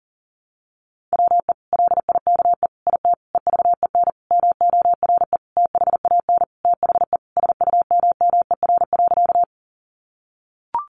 Morse code, heard as a keyed tone, about 40 wpm, 700 Hz. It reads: WE LIKE IT EVEN MORE THAN THE SUMMER.
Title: Sentences from Top 1100 Words 40wpm